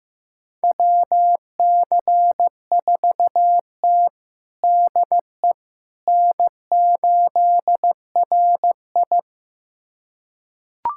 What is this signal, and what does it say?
Morse code, heard as a keyed tone, about 15 wpm, 700 Hz. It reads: WC4T DE N8RI
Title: USA Callsign DE Callsign - 4 Characters - 15wpm